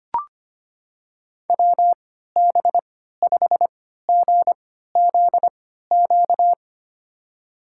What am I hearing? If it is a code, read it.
Morse code, heard as a keyed tone, about 25 wpm, spaced as for 14 wpm, 700 Hz. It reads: WB5GZQ
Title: USA Call Signs - 6 Characters - 3x Character Spacing - 25wpm